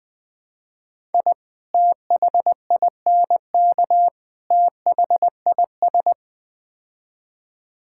Morse code, heard as a keyed tone, about 20 wpm, 700 Hz. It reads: I THINK THIS
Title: Build A Phrase - 5-Word Phrases - 20wpm